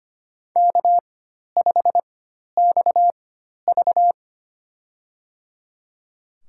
Morse code, heard as a keyed tone, about 25 wpm, spaced as for 12 wpm, 700 Hz. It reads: K5XV
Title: USA Call Signs - 4 Characters - 4x Character Spacing - 25wpm